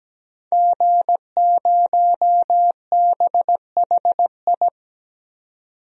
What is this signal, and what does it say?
Morse code, heard as a keyed tone, about 17 wpm, 700 Hz. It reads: G0BHI
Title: DXCC-100 Call Signs - 5 Characters - 17wpm